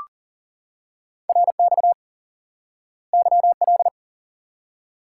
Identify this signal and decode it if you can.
Morse code, heard as a keyed tone, about 39 wpm, 700 Hz. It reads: RX YL
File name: IQE Course 3 - Lesson 89 - YAGI - Two Elements - 40wpm